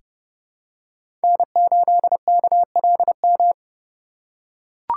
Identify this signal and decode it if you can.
Morse code, heard as a keyed tone, about 30 wpm, 700 Hz. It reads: N8KLM
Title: USA Call Signs - 5 Characters - 2x Character Spacing - 30wpm